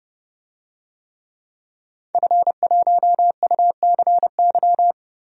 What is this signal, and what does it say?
Morse code, heard as a keyed tone, about 30 wpm, 700 Hz. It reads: F1UCY
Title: DXCC-100 Call Signs - 5 Characters - Repeated 3x - 30wpm